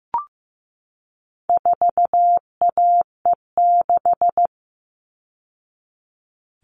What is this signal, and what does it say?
Morse code, heard as a keyed tone, about 15 wpm, 700 Hz. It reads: 4AE6
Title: All Cut Numbers - 4-Digit 15wpm